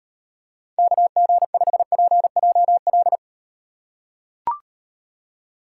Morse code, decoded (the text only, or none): KG5PJL